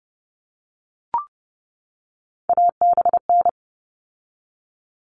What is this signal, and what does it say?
Morse code, heard as a keyed tone, about 30 wpm, 700 Hz. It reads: ABN
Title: All Cut Numbers - 3-Digit 30wpm